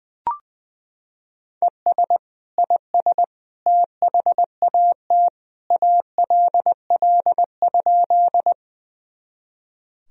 Morse code, heard as a keyed tone, about 21 wpm, 700 Hz. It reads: ES IS THAT ALL?
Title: Sentences from Top 100 Words and Abbreviations 20wpm